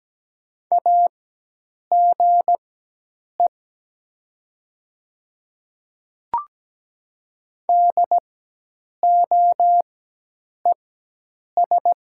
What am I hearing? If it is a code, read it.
Morse code, heard as a keyed tone, about 17 wpm, spaced as for 8 wpm, 700 Hz. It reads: AGE DOES
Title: 4x Character Spacing - Top 500 Words - 17wpm